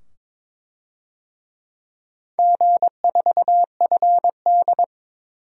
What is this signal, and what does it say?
Morse code, heard as a keyed tone, about 21 wpm, 700 Hz. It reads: G4FD
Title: DXCC-100 Call Signs - 4 Characters - 22wpm